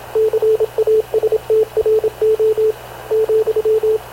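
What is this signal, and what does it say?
Morse code, heard as a keyed tone, about 27 wpm, 440 Hz, with radio noise. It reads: CASTRO,
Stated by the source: Havana Radio / CLA sending Spanish press (PX) in Morse, recorded by Radio Officer Paul Dumesnil onboard CNAV QUEST/CZDO.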